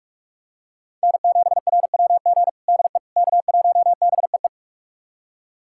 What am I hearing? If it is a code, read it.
Morse code, heard as a keyed tone, about 46 wpm, 700 Hz. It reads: N7RWG DE K1BEE
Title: USA Callsign DE Callsign - 5 Characters - 45wpm